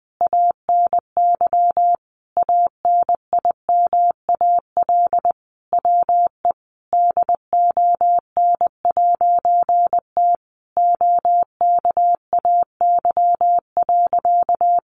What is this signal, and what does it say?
Morse code, heard as a keyed tone, about 20 wpm, 700 Hz. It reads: ANY ANIMAL WE DON'T OKAY.
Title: A Sound Of Thunder 20wpm